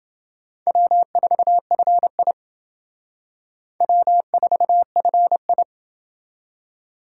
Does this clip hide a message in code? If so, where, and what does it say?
Yes: Morse code from 0.7 to 5.6 s, heard as a keyed tone, about 28 wpm, 700 Hz. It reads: W4FI W4FI